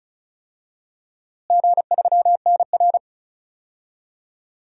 Morse code, heard as a keyed tone, about 36 wpm, 700 Hz. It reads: G3NR
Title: DXCC-100 Call Signs - 4 Characters - Repeated 3x - 35wpm